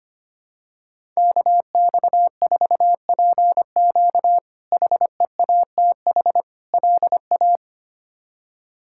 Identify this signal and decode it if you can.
Morse code, heard as a keyed tone, about 25 wpm, 700 Hz. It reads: KX4PQ HEATH LA
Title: K1USN SST QSO Practice - 25wpm